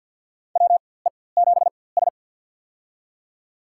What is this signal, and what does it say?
Morse code, heard as a keyed tone, about 47 wpm, spaced as for 23 wpm, 700 Hz. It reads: WE7S
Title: USA Call Signs - 4x Character Spacing - 50wpm